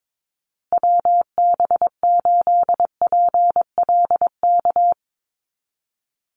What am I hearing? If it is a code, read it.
Morse code, heard as a keyed tone, about 22 wpm, 700 Hz. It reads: WB8PLK
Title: USA Call Signs Repeated 2x 22wpm